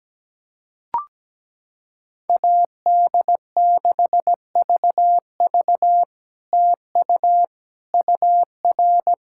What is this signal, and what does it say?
Morse code, heard as a keyed tone, about 17 wpm, 700 Hz. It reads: AD6VV TU UR
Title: SOTA QSO Practice 17wpm